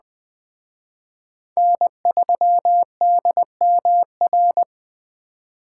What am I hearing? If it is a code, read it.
Morse code, heard as a keyed tone, about 21 wpm, 700 Hz. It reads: N3DMR